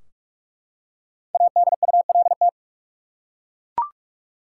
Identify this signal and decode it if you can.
Morse code, heard as a keyed tone, about 41 wpm, 700 Hz. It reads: ADULT
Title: Five Letter Words - 45wpm